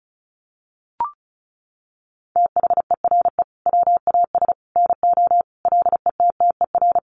Morse code, heard as a keyed tone, about 34 wpm, 700 Hz. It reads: THERE WAS NO LETTER
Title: Sentences from Top 300 Words 35wpm